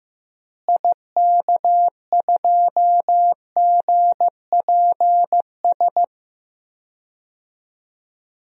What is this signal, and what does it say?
Morse code, heard as a keyed tone, about 16 wpm, 700 Hz. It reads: IK2GPS
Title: DXCC-100 Call Signs - 6 Characters - Repeated 2x - 15wpm